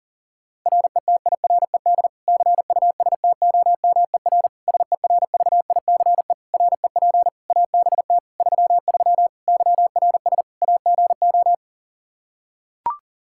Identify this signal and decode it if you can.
Morse code, heard as a keyed tone, about 37 wpm, 700 Hz. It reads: RETIRED CUSTOMER SERVICE REP ABT 33 YRS AGO
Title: Practice QSO - Exchange 3 - More Info - 40wpm